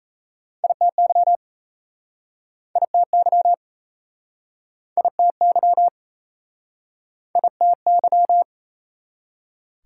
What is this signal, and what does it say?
Morse code, heard as a keyed tone, about 39 wpm, 700 Hz. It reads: ITY ITY ITY ITY